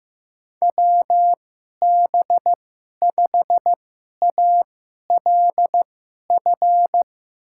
Morse code, heard as a keyed tone, about 15 wpm, spaced as for 11 wpm, 700 Hz. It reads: WB5ALF
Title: USA Call Signs - 6 Characters - 2x Character Spacing - 15wpm